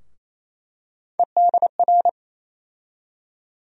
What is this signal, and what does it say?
Morse code, heard as a keyed tone, about 28 wpm, 700 Hz. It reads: EDR
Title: USA Call Sign Postfixes - 3 Letters 28wpm